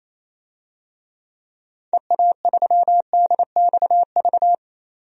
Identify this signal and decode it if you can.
Morse code, heard as a keyed tone, about 28 wpm, 700 Hz. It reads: EA3DXV
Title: DXCC-100 Call Signs - 6 Characters - 28wpm